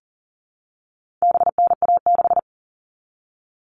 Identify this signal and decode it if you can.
Morse code, heard as a keyed tone, about 40 wpm, 700 Hz. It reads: BNA6
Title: All Cut Numbers - 4-Digit 40wpm